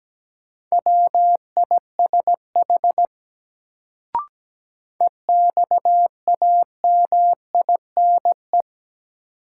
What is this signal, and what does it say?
Morse code, heard as a keyed tone, about 17 wpm, 700 Hz. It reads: WISH EXAMINE
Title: Sentences from Top 1600 Words - New Word Drill Alt - 17wpm